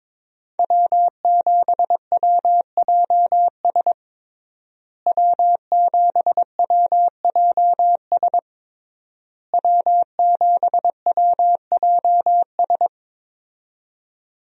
Morse code, heard as a keyed tone, about 21 wpm, 700 Hz. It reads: W7WJS W7WJS W7WJS
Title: USA Call Signs - 5 Characters - Repeated 3x - 22wpm